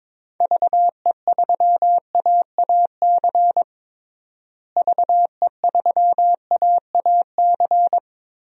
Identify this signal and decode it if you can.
Morse code, heard as a keyed tone, about 22 wpm, 700 Hz. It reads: VE3AAC VE3AAC
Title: DXCC-100 Call Signs - 6 Characters - Repeated 3x - 22wpm